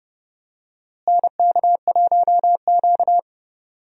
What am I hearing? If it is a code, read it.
Morse code, heard as a keyed tone, about 31 wpm, 700 Hz. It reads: NK1Q
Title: USA Call Signs - 4 Characters - Repeated 3x - 30wpm